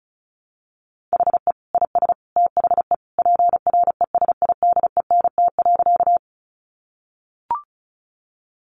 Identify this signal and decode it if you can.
Morse code, heard as a keyed tone, about 34 wpm, 700 Hz. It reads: HE IS THE PRESIDENT.
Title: Sentences from Top 1000 Words 35wpm